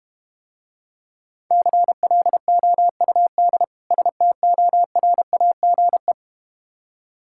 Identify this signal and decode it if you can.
Morse code, heard as a keyed tone, about 31 wpm, 700 Hz. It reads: CLOUD STORAGE
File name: practice / 25wpm / Difficult Words - Speed-Racing with Hints - 25wpm